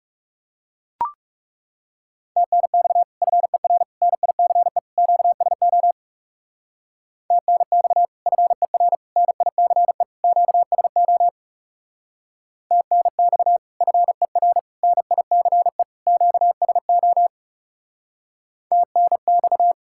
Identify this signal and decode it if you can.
Morse code, heard as a keyed tone, about 36 wpm, 700 Hz. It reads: TNX FER NICE QSO TNX FER NICE QSO TNX FER NICE QSO TNX
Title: Speed-Racing Common QSO Phrases - 30wpm